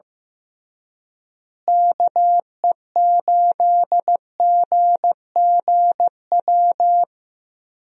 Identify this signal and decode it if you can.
Morse code, heard as a keyed tone, about 15 wpm, 700 Hz. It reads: KE8GGW
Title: USA Call Signs - 6 Characters - Repeated 3x - 15wpm